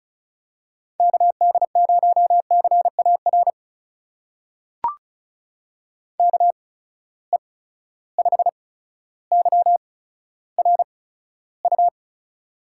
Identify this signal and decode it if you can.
Morse code, heard as a keyed tone, about 36 wpm, spaced as for 15 wpm, 700 Hz. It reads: KD0CAR KE5YRU
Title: USA Call Signs - 6 Characters - 8x Character Spacing - 35wpm